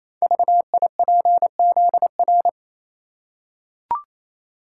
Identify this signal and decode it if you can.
Morse code, heard as a keyed tone, about 27 wpm, 700 Hz. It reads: VIPZR
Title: Code Groups - 5 Letters - 28wpm